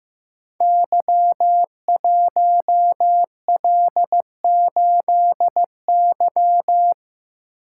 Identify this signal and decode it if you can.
Morse code, heard as a keyed tone, about 15 wpm, 700 Hz. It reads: Y1L8Y
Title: Code Groups - 5 Letter:Numbers - 15wpm